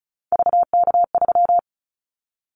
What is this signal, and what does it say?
Morse code, heard as a keyed tone, about 36 wpm, 690 Hz. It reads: VK3
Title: Callsign Prefixes for Top 50 DXCC - 35wpm